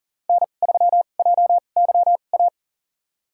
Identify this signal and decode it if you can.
Morse code, heard as a keyed tone, about 40 wpm, spaced as for 29 wpm, 700 Hz. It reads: N3JYA
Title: USA Call Signs - 2x Character Spacing - 40wpm